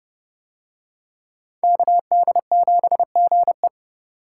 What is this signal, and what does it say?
Morse code, heard as a keyed tone, about 29 wpm, 700 Hz. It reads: KD7GE